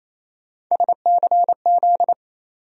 Morse code, heard as a keyed tone, about 28 wpm, 700 Hz. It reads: SCZ
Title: USA Call Sign Postfixes - 3 Letters 28wpm